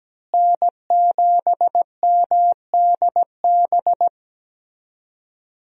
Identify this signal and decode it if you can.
Morse code, heard as a keyed tone, about 17 wpm, 700 Hz. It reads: N7MDB
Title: USA Call Signs - 5 Characters - Repeated 3x - 17wpm